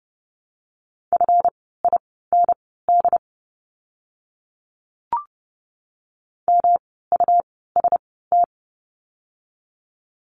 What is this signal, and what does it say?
Morse code, heard as a keyed tone, about 30 wpm, spaced as for 17 wpm, 700 Hz. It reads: FIND MUST
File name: course / 30:10wpm / 237 Lesson 38f - Letter q - Words 30wpm